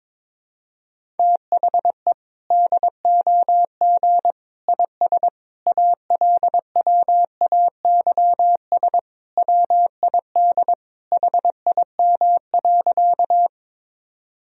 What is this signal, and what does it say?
Morse code, heard as a keyed tone, about 22 wpm, 700 Hz. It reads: THE DOG IS ALWAYS WID HIM.
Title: Sentences from Top 400 Words and Abbreviations 22wpm